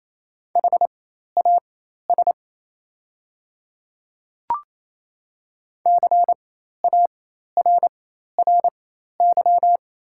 Morse code, heard as a keyed tone, about 28 wpm, spaced as for 13 wpm, 700 Hz. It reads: HAS CARRY
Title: 4x Character Spacing - Top 300 Words - 28wpm